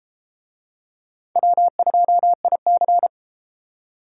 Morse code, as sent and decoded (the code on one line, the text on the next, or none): .-- ..--- .. -.-.
W2IC